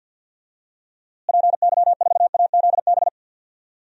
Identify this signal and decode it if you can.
Morse code, heard as a keyed tone, about 49 wpm, 700 Hz. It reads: PY4AZB
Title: DXCC-100 Call Signs - 6 Characters - Repeated 3x - 50wpm